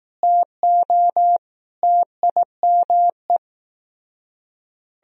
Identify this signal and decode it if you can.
Morse code, heard as a keyed tone, about 18 wpm, 700 Hz. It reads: TO TIME
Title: Speed-Racing - Sets of 4 Words 15wpm